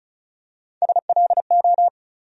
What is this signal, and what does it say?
Morse code, heard as a keyed tone, about 34 wpm, 700 Hz. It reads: SLO
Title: USA Call Sign Postfixes - 3 Letters 35wpm